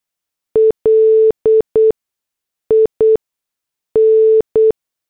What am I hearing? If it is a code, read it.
Morse code, heard as a keyed tone, about 8 wpm, spaced as for 6 wpm, 440 Hz. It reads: LIN